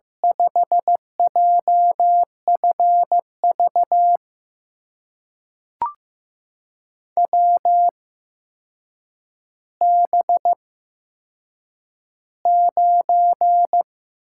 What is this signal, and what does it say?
Morse code, heard as a keyed tone, about 15 wpm, spaced as for 7 wpm, 700 Hz. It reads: 5JFV WB9
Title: USA Call Signs - 8x Character Spacing - 15wpm